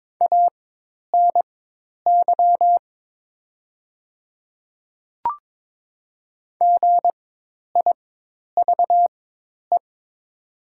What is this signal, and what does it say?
Morse code, heard as a keyed tone, about 22 wpm, spaced as for 10 wpm, 700 Hz. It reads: ANY GIVE
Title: 4x Character Spacing - Top 500 Words - 22wpm